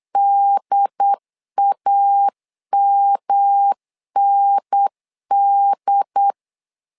Morse code, heard as a keyed tone, about 8 wpm, 800 Hz. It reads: DAMND
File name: LEZIONI / 3 MWRO / 3 M 8wpm 2300